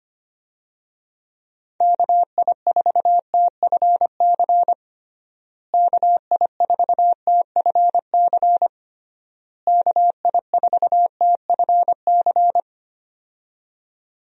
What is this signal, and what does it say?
Morse code, heard as a keyed tone, about 25 wpm, 700 Hz. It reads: KI4TFC KI4TFC KI4TFC
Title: USA Call Signs - 6 Characters - Repeated 3x - 25wpm